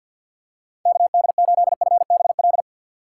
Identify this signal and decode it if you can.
Morse code, heard as a keyed tone, about 46 wpm, 690 Hz. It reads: KD8FBL
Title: USA Call Signs - 6 Characters - 3x Character Spacing - 50wpm